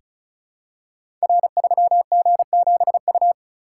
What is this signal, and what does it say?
Morse code, heard as a keyed tone, about 36 wpm, 700 Hz. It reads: R3G7U
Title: Code Groups - 5 Letter:Numbers - 35wpm